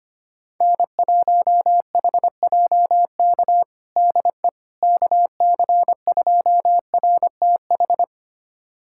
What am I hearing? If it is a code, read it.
Morse code, heard as a keyed tone, about 25 wpm, 700 Hz. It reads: N1HJK DE KC2RTH
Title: USA Callsign DE Callsign 25wpm